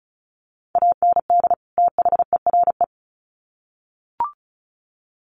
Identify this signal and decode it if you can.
Morse code, heard as a keyed tone, about 34 wpm, 700 Hz. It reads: AND THERE
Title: Sets of 2 Words 35wpm